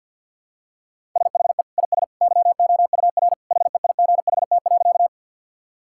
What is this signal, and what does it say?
Morse code, heard as a keyed tone, about 50 wpm, 700 Hz. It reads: SHE IS YOUR HEIGHT.
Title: Sentences from Top 1600 Words 50wpm